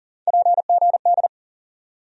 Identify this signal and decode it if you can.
Morse code, heard as a keyed tone, about 40 wpm, 700 Hz. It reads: PGD
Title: USA Call Sign Postfixes - 3 Letters 40wpm